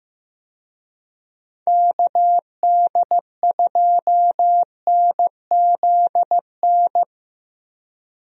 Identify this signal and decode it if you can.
Morse code, heard as a keyed tone, about 15 wpm, 700 Hz. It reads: KD2NZN
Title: USA Call Signs - 6 Characters - Repeated 2x - 15wpm